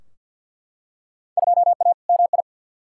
Morse code, heard as a keyed tone, about 52 wpm, 700 Hz. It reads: 2A MI
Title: Field Day Exchanges - 50wpm